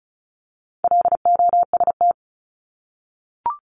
Morse code, heard as a keyed tone, about 34 wpm, 700 Hz. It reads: LOST
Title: Top 1251-1500 Words 35wpm